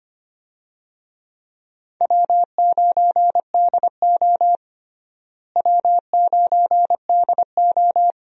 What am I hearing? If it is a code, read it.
Morse code, heard as a keyed tone, about 25 wpm, 700 Hz. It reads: W9DO W9DO
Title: USA Call Signs - 4 Characters - Repeated 3x - 25wpm